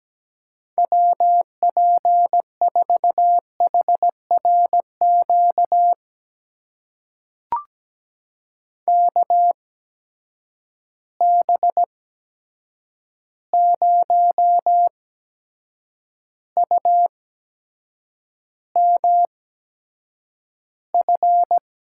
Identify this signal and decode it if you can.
Morse code, heard as a keyed tone, about 17 wpm, spaced as for 7 wpm, 700 Hz. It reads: WP4HRQ KB0UMF
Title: USA Call Signs - 6 Characters - 8x Character Spacing - 17wpm